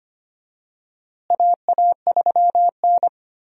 Morse code, heard as a keyed tone, about 25 wpm, 700 Hz. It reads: AA3N
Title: USA Call Signs - 4 Characters - Repeated 2x - 25wpm